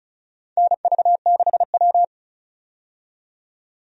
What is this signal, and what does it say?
Morse code, heard as a keyed tone, about 36 wpm, 700 Hz. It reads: NV6W
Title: USA Call Signs - 4 Characters - 35wpm